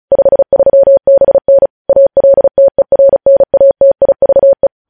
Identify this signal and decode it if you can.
Morse code, heard as a keyed tone, about 36 wpm, 550 Hz. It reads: 53BN ALTERNATIVE
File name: podcast / News headlines-1x-35x35